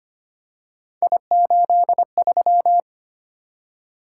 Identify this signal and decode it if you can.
Morse code, heard as a keyed tone, about 25 wpm, 700 Hz. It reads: I83
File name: Three Letter-Number Combinations - 25wpm